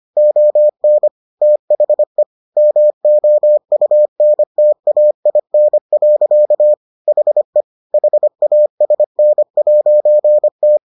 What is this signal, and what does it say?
Morse code, heard as a keyed tone, about 25 wpm, 600 Hz. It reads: ON THE MOUNTAIN. HE HASN'T